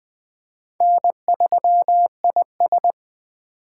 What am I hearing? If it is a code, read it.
Morse code, heard as a keyed tone, about 24 wpm, 700 Hz. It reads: N3IS